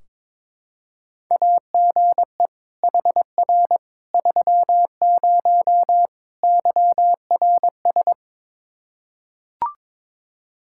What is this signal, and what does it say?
Morse code, heard as a keyed tone, about 22 wpm, 700 Hz. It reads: AGE HR 30 YRS
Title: Practice QSO - Exchange 3 - More Info - 22wpm